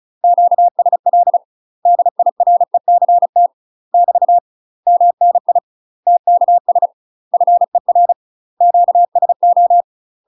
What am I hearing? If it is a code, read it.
Morse code, heard as a keyed tone, about 35 wpm, 700 Hz. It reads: QSL DIRECT <BT> MNI TKS FER QSO
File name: UK MASTER QSOs TRAINING / 35WPM / UK MASTER QSOs 0005